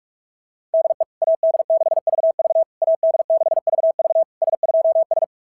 Morse code, heard as a keyed tone, about 44 wpm, 690 Hz. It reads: DE AD6VV AD6VV S2S
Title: SOTA QSO Practice 45wpm